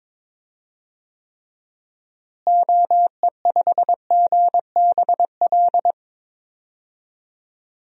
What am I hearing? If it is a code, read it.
Morse code, heard as a keyed tone, about 22 wpm, 700 Hz. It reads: OE5GBL
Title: DXCC-100 Call Signs - 6 Characters - Repeated 2x - 22wpm